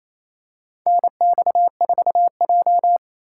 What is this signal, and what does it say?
Morse code, heard as a keyed tone, about 28 wpm, 700 Hz. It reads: NX4J
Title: USA Call Signs - 4 Characters - 2x Character Spacing - 28wpm